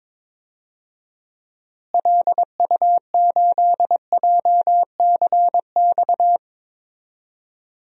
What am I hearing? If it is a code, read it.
Morse code, heard as a keyed tone, about 22 wpm, 700 Hz. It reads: LU8JCX
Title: DXCC-100 Call Signs - 6 Characters - Repeated 3x - 22wpm